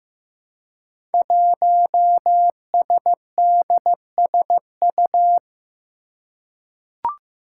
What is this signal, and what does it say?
Morse code, heard as a keyed tone, about 15 wpm, 700 Hz. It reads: 1SDSU
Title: Five Challenging Letter-Number Combinations 15wpm